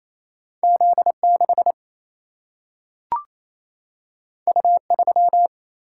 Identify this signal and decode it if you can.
Morse code, heard as a keyed tone, about 28 wpm, 700 Hz. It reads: Z6 U3
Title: Single-Letter Single-Number 28wpm